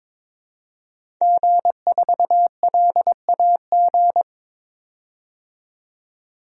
Morse code, heard as a keyed tone, about 22 wpm, 700 Hz. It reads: G4LAG
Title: DXCC-100 Call Signs - 5 Characters - 22wpm